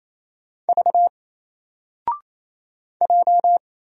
Morse code, heard as a keyed tone, about 28 wpm, 700 Hz. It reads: V J
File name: Single-Letter-Number-28wpm